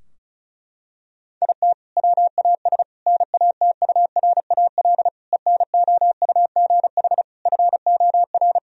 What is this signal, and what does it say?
Morse code, heard as a keyed tone, about 34 wpm, 700 Hz. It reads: IT WAS NATURAL ENOUGH FOR
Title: Sentences from Top 1000 Words - Review Part 4 - 35wpm